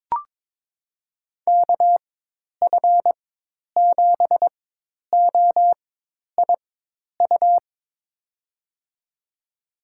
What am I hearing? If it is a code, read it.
Morse code, heard as a keyed tone, about 22 wpm, spaced as for 10 wpm, 700 Hz. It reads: KF7OIU
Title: USA Call Signs - 6 Characters - 4x Character Spacing - 22wpm